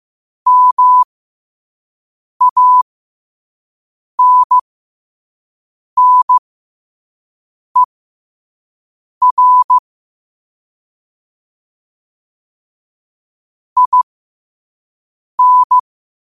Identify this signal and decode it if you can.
Morse code, heard as a keyed tone, about 15 wpm, spaced as for 5 wpm, 1000 Hz. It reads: MANNER IN
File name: qotdfiles / 5wpm